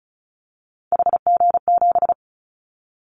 Morse code, heard as a keyed tone, about 36 wpm, 700 Hz. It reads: HG7
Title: Callsign Prefixes for Top 50 DXCC - 35wpm